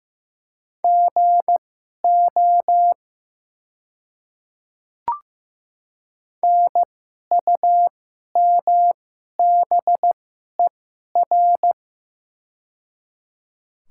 Morse code, heard as a keyed tone, about 15 wpm, spaced as for 11 wpm, 700 Hz. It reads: GO NUMBER
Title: 2x Character Spacing - Top 200 Words - 15wpm